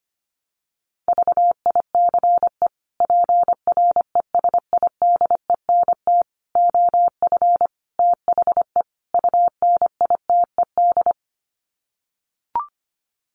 Morse code, heard as a keyed tone, about 25 wpm, 700 Hz. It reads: VICE PRESIDENT OF THE UNITED
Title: Sets of 5 Words Encore2 25wpm